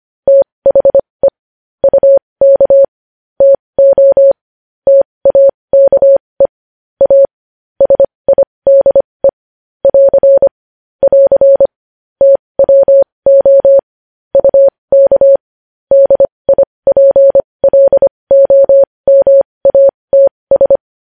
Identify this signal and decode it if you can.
Morse code, heard as a keyed tone, about 25 wpm, spaced as for 20 wpm, 550 Hz. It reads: THE UK TO TAKE A SIDE <AR> <AR> TWO UK DIPLOMATS
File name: podcast / News headlines-1x-25x20